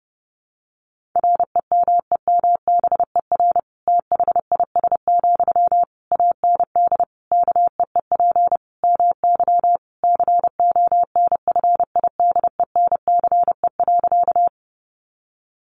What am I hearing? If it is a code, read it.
Morse code, heard as a keyed tone, about 30 wpm, 700 Hz. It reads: REMEMBER THIS, AND KEEP MY CONFIDENCE.
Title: Sentences from Top 1500 Words 30wpm